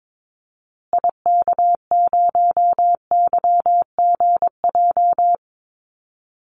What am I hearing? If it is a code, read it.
Morse code, heard as a keyed tone, about 22 wpm, 700 Hz. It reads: IK0YGJ
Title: Call Signs Repeated 2x - 22wpm